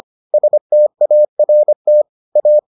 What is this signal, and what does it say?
Morse code, heard as a keyed tone, about 25 wpm, 600 Hz. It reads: START A